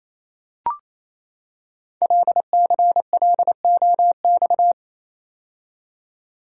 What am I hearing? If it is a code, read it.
Morse code, heard as a keyed tone, about 28 wpm, 700 Hz. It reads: LCLOX
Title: Code Groups - 5 Letter:Numbers - 28wpm